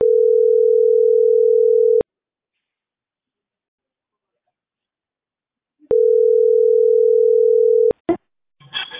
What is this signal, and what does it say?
Ringback tone (North America), 440 + 480 Hz.